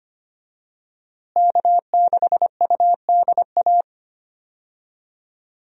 Morse code, heard as a keyed tone, about 25 wpm, 700 Hz. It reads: K6UDA
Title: USA Call Signs - 5 Characters - Repeated 2x - 25wpm